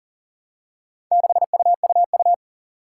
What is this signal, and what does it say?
Morse code, heard as a keyed tone, about 41 wpm, 700 Hz. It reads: 6UUU
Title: Four Challenging Letter-Number Combinations 40wpm